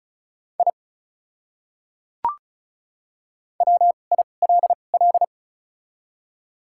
Morse code, heard as a keyed tone, about 36 wpm, 700 Hz. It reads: I W I L L
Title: 2x Character Spacing - Top 200 Words - 35wpm